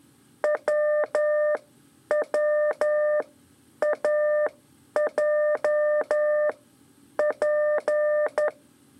Morse code, heard as a keyed tone, about 11 wpm, spaced as for 8 wpm, 600 Hz. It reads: WWAJP
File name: 3 lezione CW 2 AIR